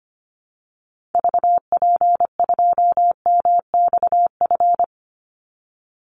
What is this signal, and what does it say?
Morse code, heard as a keyed tone, about 25 wpm, 700 Hz. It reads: VP2MXF
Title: Call Signs Repeated 3x - 25wpm